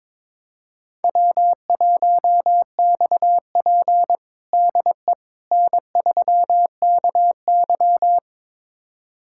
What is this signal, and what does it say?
Morse code, heard as a keyed tone, about 22 wpm, 700 Hz. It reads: W1XP DE N3KY
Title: USA Callsign DE Callsign - 4 Characters - 22wpm